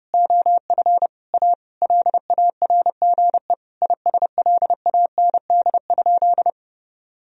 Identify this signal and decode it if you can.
Morse code, heard as a keyed tone, about 30 wpm, 700 Hz. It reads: OF A LARGE ISLAND?
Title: Sentences from Top 1000 Words - Review Part 3 - 30wpm